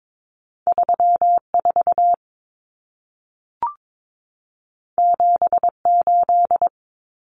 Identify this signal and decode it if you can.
Morse code, heard as a keyed tone, about 22 wpm, 700 Hz. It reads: 34 78
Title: Numbers - 2-Digit 22wpm